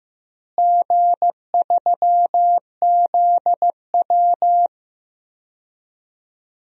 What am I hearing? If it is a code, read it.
Morse code, heard as a keyed tone, about 15 wpm, 700 Hz. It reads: G3ZW
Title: DXCC-100 Call Signs - 4 Characters - Repeated 3x - 15wpm